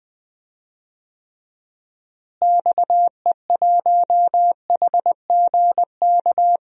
Morse code, heard as a keyed tone, about 19 wpm, 700 Hz. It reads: XE1HGK
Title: DXCC-100 Call Signs - 6 Characters - Repeated 3x - 20wpm